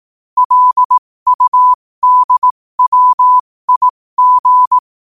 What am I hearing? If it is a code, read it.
Morse code, heard as a keyed tone, about 18 wpm, spaced as for 15 wpm, 1000 Hz. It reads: LUDWIG
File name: qotdfiles / 15wpm